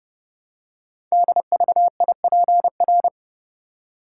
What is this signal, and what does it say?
Morse code, heard as a keyed tone, about 28 wpm, 700 Hz. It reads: DVIPR
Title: Code Groups - 5 Letters - 30wpm